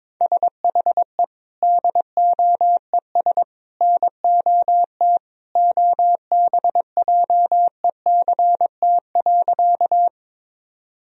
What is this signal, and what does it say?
Morse code, heard as a keyed tone, about 22 wpm, 700 Hz. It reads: SHE DOES NOT OBJECT.
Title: Sentences from Top 1000 Words - Review Part 3 - 22wpm